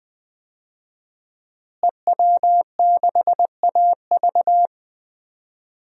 Morse code, heard as a keyed tone, about 21 wpm, 700 Hz. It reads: EW6AV
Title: DXCC-100 Call Signs - 5 Characters - Repeated 2x - 20wpm